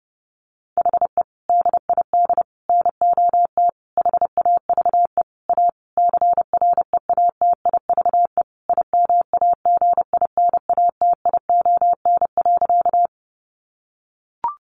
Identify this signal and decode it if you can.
Morse code, heard as a keyed tone, about 30 wpm, 700 Hz. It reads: HE DID NOT HAVE A CREATIVE IMAGINATION.
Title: Sentences from Top 1300 Words 30wpm